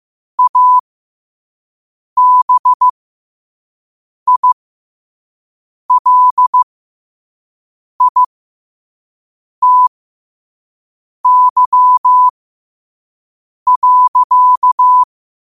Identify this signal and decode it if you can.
Morse code, heard as a keyed tone, about 15 wpm, spaced as for 5 wpm, 1000 Hz. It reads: ABILITY.
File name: qotdfiles / 5wpm